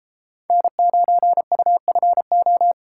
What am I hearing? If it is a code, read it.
Morse code, heard as a keyed tone, about 33 wpm, 700 Hz. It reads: N9UFO